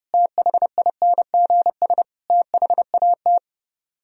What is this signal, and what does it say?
Morse code, heard as a keyed tone, about 33 wpm, 700 Hz. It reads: THINGS THAT